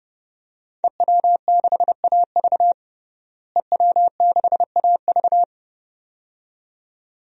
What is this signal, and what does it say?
Morse code, heard as a keyed tone, about 30 wpm, 700 Hz. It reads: EW6AV EW6AV
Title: DXCC-100 Call Signs - 5 Characters - Repeated 2x - 30wpm